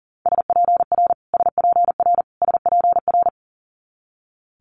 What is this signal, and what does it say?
Morse code, heard as a keyed tone, about 38 wpm, 700 Hz. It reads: SPR SPR SPR
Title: Multi-letter Phonemes Repeated 3x 40wpm